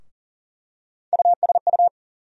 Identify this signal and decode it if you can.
Morse code, heard as a keyed tone, about 40 wpm, 700 Hz. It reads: USU
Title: USA Call Sign Postfixes - 3 Letters 40wpm